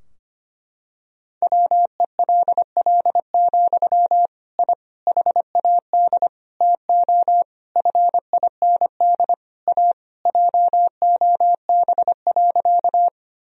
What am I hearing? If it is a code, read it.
Morse code, heard as a keyed tone, about 25 wpm, 700 Hz. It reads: WELL, I HAD TO FIND A JOB.
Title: Sentences from Top 1000 Words - Review Part 3 - 25wpm